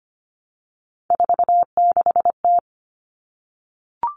Morse code, heard as a keyed tone, about 25 wpm, 700 Hz. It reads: 46T
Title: All Cut Numbers - 3-Digit 25wpm